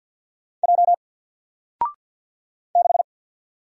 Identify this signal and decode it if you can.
Morse code, heard as a keyed tone, about 53 wpm, 700 Hz. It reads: J 6
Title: Single Letter-Number 50wpm